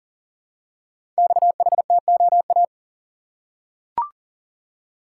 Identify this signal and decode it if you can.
Morse code, heard as a keyed tone, about 40 wpm, 700 Hz. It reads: XHTOA
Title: Code Groups - 5 Letters - 40wpm